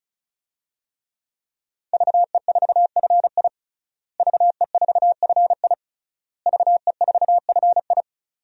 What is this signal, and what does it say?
Morse code, heard as a keyed tone, about 35 wpm, 700 Hz. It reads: VE4FI VE4FI VE4FI
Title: DXCC-100 Call Signs - 5 Characters - Repeated 3x - 35wpm